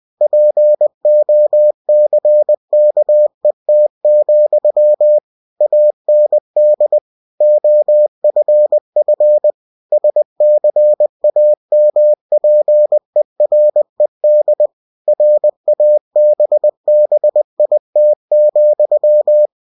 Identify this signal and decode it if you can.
Morse code, heard as a keyed tone, about 20 wpm, 600 Hz. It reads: POCKET, AND OFF SCAMPERED RABBIT,
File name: cw-books / winnie-the-pooh / winnie-20wpm-page-0069